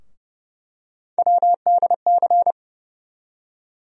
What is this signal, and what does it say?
Morse code, heard as a keyed tone, about 29 wpm, 700 Hz. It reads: WDC
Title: USA Call Sign Postfixes - 3 Letters 30wpm